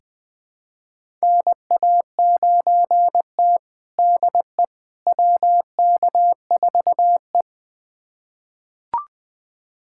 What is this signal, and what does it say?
Morse code, heard as a keyed tone, about 20 wpm, 700 Hz. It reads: NA9T DE WK4E
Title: USA Callsign DE Callsign - 4 Characters - 20wpm